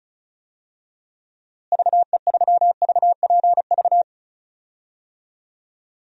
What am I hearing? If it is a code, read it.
Morse code, heard as a keyed tone, about 34 wpm, 700 Hz. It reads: VE3VPV
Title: DXCC-100 Call Signs - 6 Characters - Repeated 3x - 35wpm